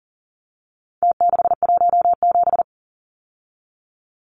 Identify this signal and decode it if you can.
Morse code, heard as a keyed tone, about 40 wpm, 700 Hz. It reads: T617
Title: Numbers - Contest Style - 4-Digit 40wpm